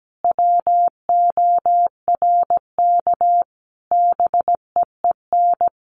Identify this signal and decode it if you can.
Morse code, heard as a keyed tone, about 17 wpm, 700 Hz. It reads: WORK BEEN
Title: Sentences from Top 1500 Words 17wpm